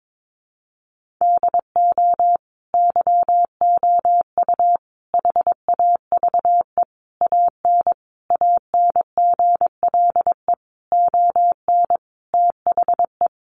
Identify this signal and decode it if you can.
Morse code, heard as a keyed tone, about 22 wpm, 700 Hz. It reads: DO YOU HAVE AN ANGLE ON THE
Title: Sentences from Top 1500 Words 22wpm